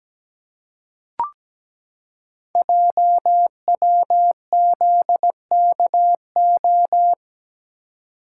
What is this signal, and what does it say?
Morse code, heard as a keyed tone, about 17 wpm, 700 Hz. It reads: JWZKO
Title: Code Groups - 5 Letters - 17wpm